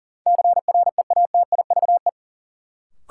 Morse code, heard as a keyed tone, about 39 wpm, 700 Hz. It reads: CREATIVE
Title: Sentences from Top 1000 Words - Review Part 5 - New Word Drill - 40wpm